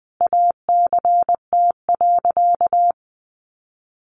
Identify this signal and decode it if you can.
Morse code, heard as a keyed tone, about 20 wpm, 700 Hz. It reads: ACT.
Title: Sentences from Top 1400 Words 20wpm